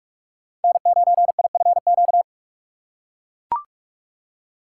Morse code, heard as a keyed tone, about 43 wpm, 700 Hz. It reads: N9IFQ
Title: USA Call Signs - 3x Character Spacing - 45wpm